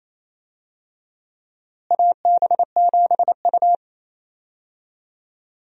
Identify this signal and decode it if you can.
Morse code, heard as a keyed tone, about 28 wpm, 700 Hz. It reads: AB7U
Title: USA Call Signs - 4 Characters - Repeated 3x - 28wpm